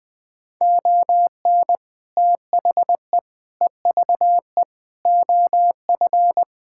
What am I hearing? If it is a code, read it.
Morse code, heard as a keyed tone, about 21 wpm, 700 Hz. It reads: ON THE EVE OF
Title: Sentences with 5-Word Phrases - 20wpm